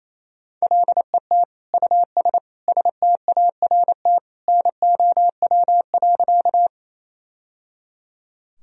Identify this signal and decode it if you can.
Morse code, heard as a keyed tone, about 28 wpm, 700 Hz. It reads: LET US START NOW.
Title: Sentences from Top 1000 Words - Review Part 1 - 28wpm